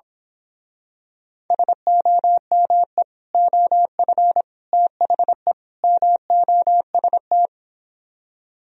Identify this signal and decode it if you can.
Morse code, heard as a keyed tone, about 26 wpm, 700 Hz. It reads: SOME OF THE MOST
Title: Speed-Racing - Sets of 4 Words 22wpm